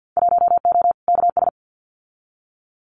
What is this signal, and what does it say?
Morse code, heard as a keyed tone, about 50 wpm, 700 Hz. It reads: 1O KS
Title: Winter Field Day Exchanges 50wpm